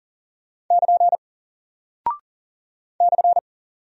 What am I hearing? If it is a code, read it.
Morse code, heard as a keyed tone, about 40 wpm, 700 Hz. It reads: <KN> /
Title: Single Character (Letter:Number:Punctuation:Prosign) Alt 40wpm